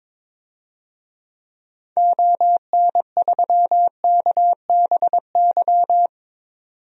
Morse code, heard as a keyed tone, about 22 wpm, 700 Hz. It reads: ON3KBY
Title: DXCC-100 Call Signs - 6 Characters - 22wpm